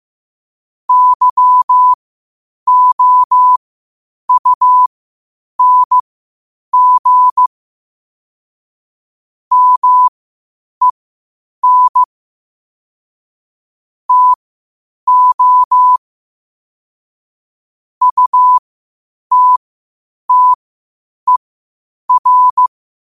Morse code, heard as a keyed tone, about 15 wpm, spaced as for 8 wpm, 1000 Hz. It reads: YOUNG MEN TO UTTER
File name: qotdfiles / 8wpm